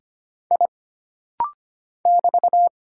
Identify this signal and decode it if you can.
Morse code, heard as a keyed tone, about 25 wpm, 700 Hz. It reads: I <BT>
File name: Single Character (Letter:Number:Punctuation:Prosign) - Rapid-Fire - 25wpm